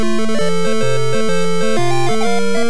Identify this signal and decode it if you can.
There is no code.